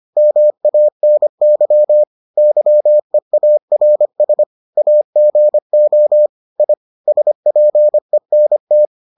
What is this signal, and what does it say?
Morse code, heard as a keyed tone, about 25 wpm, 600 Hz. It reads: MANY YEARS AGO I SPENT